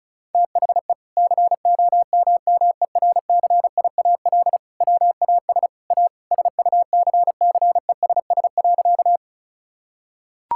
Morse code, heard as a keyed tone, about 34 wpm, 700 Hz. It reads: THE COMMERCIAL WAS A SUCCESS.
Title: Sentences from Top 1000 Words - Review Part 5 - 35wpm